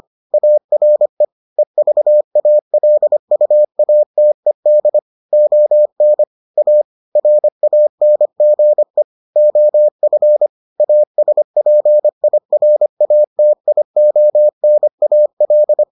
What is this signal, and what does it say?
Morse code, heard as a keyed tone, about 25 wpm, 600 Hz. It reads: ARE EVALUATED ON A RANGE OF ASPIRATIONAL